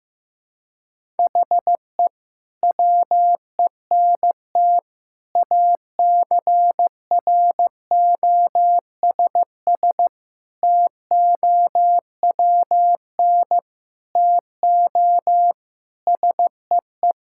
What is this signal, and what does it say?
Morse code, heard as a keyed tone, about 15 wpm, 700 Hz. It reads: HE WENT ACROSS TOWN TO SEE
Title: Sentences from Top 1000 Words - Review Part 5 - 15wpm